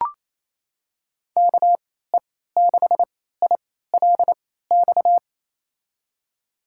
Morse code, heard as a keyed tone, about 28 wpm, spaced as for 16 wpm, 700 Hz. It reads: KE6ILX
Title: USA Call Signs - 3x Character Spacing - 28wpm